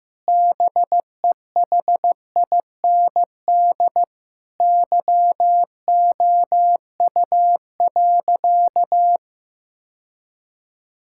Morse code, heard as a keyed tone, about 15 wpm, 700 Hz. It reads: BEHIND YOU.
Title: Sentences from Top 1000 Words - Review Part 6 - 15wpm